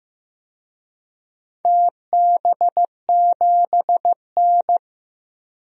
Morse code, heard as a keyed tone, about 15 wpm, 700 Hz. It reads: TB7N
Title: DXCC-100 Call Signs - 4 Characters - 15wpm